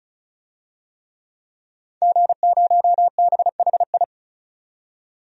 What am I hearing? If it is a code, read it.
Morse code, heard as a keyed tone, about 36 wpm, 700 Hz. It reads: G0BHI
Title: DXCC-100 Call Signs - 5 Characters - 35wpm